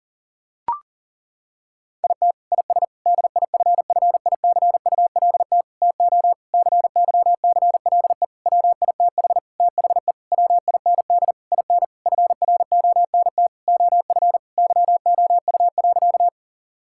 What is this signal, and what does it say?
Morse code, heard as a keyed tone, about 40 wpm, 700 Hz. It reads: IT IS DIFFICULT TO CYCLE WITH THE WIND IN FRONT OF YOU.
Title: Sentences from Top 1000 Words - Review Part 6 - 40wpm